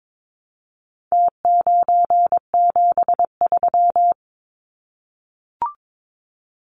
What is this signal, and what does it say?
Morse code, heard as a keyed tone, about 22 wpm, 700 Hz. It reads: T973
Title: Numbers - Contest Style - 4-Digit 22wpm